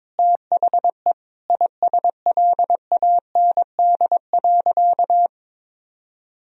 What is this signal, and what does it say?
Morse code, heard as a keyed tone, about 22 wpm, 700 Hz. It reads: THE ISLAND.
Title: Sentences from Top 1000 Words - Review Part 3 - 22wpm